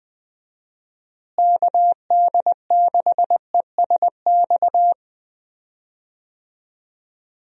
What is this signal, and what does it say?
Morse code, heard as a keyed tone, about 20 wpm, 700 Hz. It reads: KD6ESX
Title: USA Call Signs - 6 Characters - Repeated 2x - 20wpm